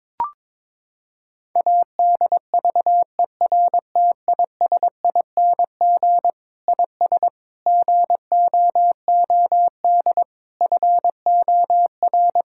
Morse code, heard as a keyed tone, about 22 wpm, 700 Hz. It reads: ADVERTISING IS GOOD FOR
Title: Sentences from Top 1000 Words - Review Part 6 - 22wpm